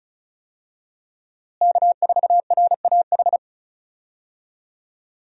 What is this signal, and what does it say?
Morse code, heard as a keyed tone, about 36 wpm, 700 Hz. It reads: K4RAH
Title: USA Call Signs - 5 Characters - Repeated 3x - 35wpm